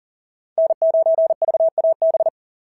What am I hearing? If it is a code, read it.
Morse code, heard as a keyed tone, about 40 wpm, 650 Hz. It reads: N9VAB
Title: SOTA QSO Practice 40wpm